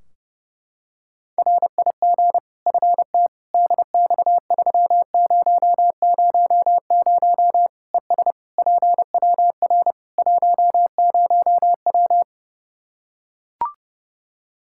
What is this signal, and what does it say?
Morse code, heard as a keyed tone, about 30 wpm, 700 Hz. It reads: RIG FT DX3000 ES PWR 10W
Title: Practice QSO - Exchange 2 - Info - 30wpm